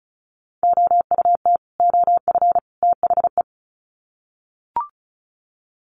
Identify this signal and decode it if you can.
Morse code, heard as a keyed tone, about 34 wpm, 700 Hz. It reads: OUT OF THE
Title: Sets of 3 Words 35wpm